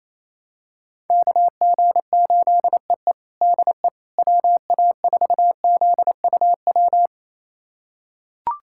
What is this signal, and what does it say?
Morse code, heard as a keyed tone, about 27 wpm, 700 Hz. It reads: KG8EE DE WA4ZUW
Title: USA Callsign DE Callsign 28wpm